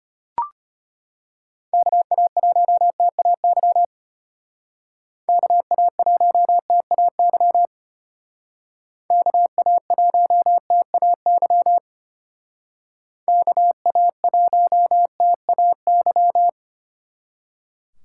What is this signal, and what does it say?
Morse code, heard as a keyed tone, about 31 wpm, 700 Hz. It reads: KA1TAY KA1TAY KA1TAY KA1TAY